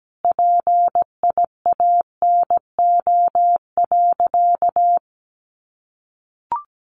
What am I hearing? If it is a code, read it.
Morse code, heard as a keyed tone, about 17 wpm, 700 Hz. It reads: PIANO.
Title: Additional Easy Phrases 17wpm